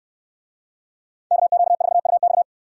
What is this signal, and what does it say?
Morse code, heard as a keyed tone, about 69 wpm, 700 Hz. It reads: KQ4UX